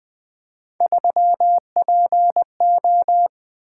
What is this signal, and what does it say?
Morse code, heard as a keyed tone, about 21 wpm, 700 Hz. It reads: 3PO
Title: Three Letter-Number Combinations - 20wpm